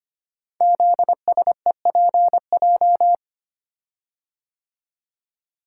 Morse code, heard as a keyed tone, about 25 wpm, 700 Hz. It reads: ZSEPJ
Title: Code Groups - 5 Letters - 25wpm